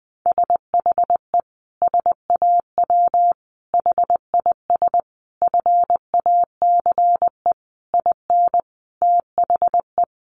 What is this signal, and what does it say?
Morse code, heard as a keyed tone, about 20 wpm, 700 Hz. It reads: SHE SAW HIS FACE IN THE
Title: Sentences from Top 1300 Words 20wpm